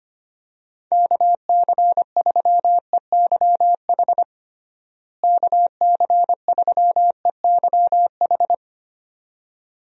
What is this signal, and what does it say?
Morse code, heard as a keyed tone, about 25 wpm, 700 Hz. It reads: KC3EYH KC3EYH
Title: USA Call Signs - 6 Characters - Repeated 2x - 25wpm